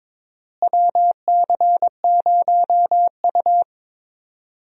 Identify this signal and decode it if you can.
Morse code, heard as a keyed tone, about 22 wpm, 700 Hz. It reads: WC0U
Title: USA Call Signs - 4 Characters - Repeated 2x - 22wpm